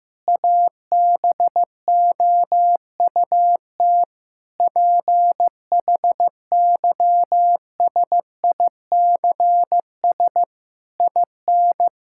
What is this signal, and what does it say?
Morse code, heard as a keyed tone, about 15 wpm, 700 Hz. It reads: ABOUT PHYSICS IN
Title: Sentences from Top 1000 Words - Review Part 5 - 15wpm